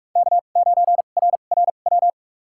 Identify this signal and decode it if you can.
Morse code, heard as a keyed tone, about 43 wpm, spaced as for 32 wpm, 700 Hz. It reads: K9RRW
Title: USA Call Signs - 5 Characters - 2x Character Spacing - 45wpm